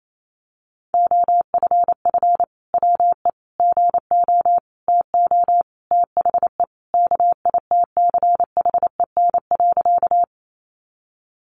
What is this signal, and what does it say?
Morse code, heard as a keyed tone, about 28 wpm, 700 Hz. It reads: OFF WE GO TO THE KITCHEN.
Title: Sentences from Top 1400 Words 28wpm